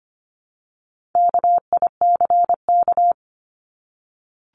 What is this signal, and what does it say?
Morse code, heard as a keyed tone, about 25 wpm, 700 Hz. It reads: KICK
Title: Top 1251-1500 Words 25wpm